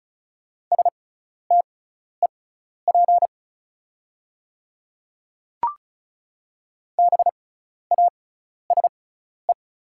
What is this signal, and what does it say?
Morse code, heard as a keyed tone, about 34 wpm, spaced as for 12 wpm, 690 Hz. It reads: STEP BASE
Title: 6x Character Spacing - Top 500 Words - 35wpm